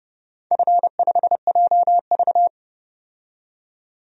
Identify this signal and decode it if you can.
Morse code, heard as a keyed tone, about 29 wpm, 700 Hz. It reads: F5JV
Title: DXCC-100 Call Signs - 4 Characters - 30wpm